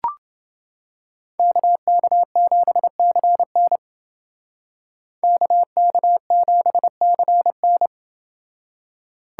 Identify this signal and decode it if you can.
Morse code, heard as a keyed tone, about 29 wpm, 700 Hz. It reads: KK7CN KK7CN